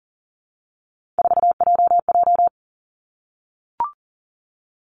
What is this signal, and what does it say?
Morse code, heard as a keyed tone, about 40 wpm, 700 Hz. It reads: 4JJ
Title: Three Challenging Letter-Number Combinations 40wpm